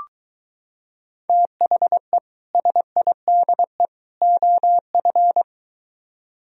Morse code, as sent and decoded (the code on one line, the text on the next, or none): - .... / ... .. -.. / --- ..-.
THE SIDE OF